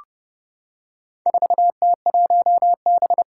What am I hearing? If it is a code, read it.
Morse code, heard as a keyed tone, about 31 wpm, 700 Hz. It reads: 4T1B
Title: Four Challenging Letter-Number Combinations 30wpm